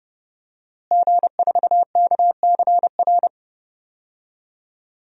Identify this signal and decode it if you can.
Morse code, heard as a keyed tone, about 29 wpm, 700 Hz. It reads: G4KCR
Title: DXCC-100 Call Signs - 5 Characters - Repeated 2x - 30wpm